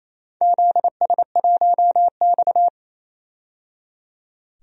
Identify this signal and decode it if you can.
Morse code, heard as a keyed tone, about 29 wpm, 700 Hz. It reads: ZS1X
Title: DXCC-100 Call Signs - 4 Characters - 28wpm